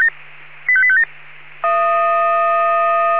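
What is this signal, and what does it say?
Five-tone selective call (ZVEI), heard as quick runs of five tones, tones of about 70 ms. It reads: 85857
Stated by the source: ZVEI five-tone call 85857 received on SDR.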